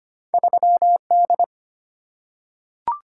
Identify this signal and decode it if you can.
Morse code, heard as a keyed tone, about 25 wpm, 690 Hz. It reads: 3D
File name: Two Letter-Number Combinations - 25wpm